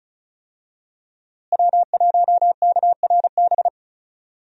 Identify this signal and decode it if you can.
Morse code, heard as a keyed tone, about 34 wpm, 700 Hz. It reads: W1KRB
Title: USA Call Signs - 5 Characters - Repeated 2x - 35wpm